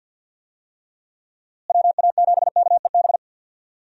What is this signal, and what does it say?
Morse code, heard as a keyed tone, about 51 wpm, 700 Hz. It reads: WA7KEB
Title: USA Call Signs - 6 Characters - Repeated 2x - 50wpm